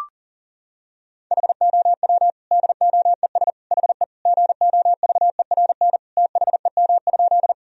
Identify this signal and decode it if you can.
Morse code, heard as a keyed tone, about 40 wpm, 700 Hz. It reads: HOW DOES HE GOVERN THEM?
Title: Sentences from Top 1000 Words - Review Part 2 - 40wpm